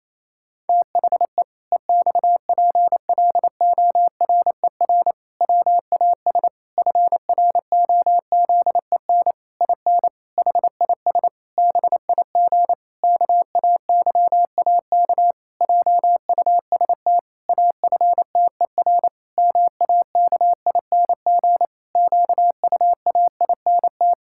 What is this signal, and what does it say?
Morse code, heard as a keyed tone, about 28 wpm, 700 Hz. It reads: THE EXPLORER WAS FROZEN IN HIS BIG KAYAK JUST AFTER MAKING QUAINT
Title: Pangram Challenge v2 - 28wpm